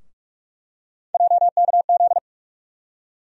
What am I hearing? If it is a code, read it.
Morse code, heard as a keyed tone, about 46 wpm, 700 Hz. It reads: JKZ
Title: USA Call Sign Postfixes - 3 Letters 45wpm